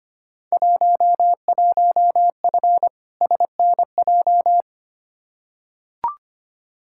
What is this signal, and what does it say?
Morse code, heard as a keyed tone, about 25 wpm, 700 Hz. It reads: 11F SNJ
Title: Field Day Exchanges Encore - 25wpm